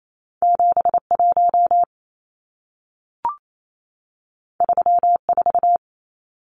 Morse code, heard as a keyed tone, about 28 wpm, 700 Hz. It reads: 71 34
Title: Numbers - 2-Digit 28wpm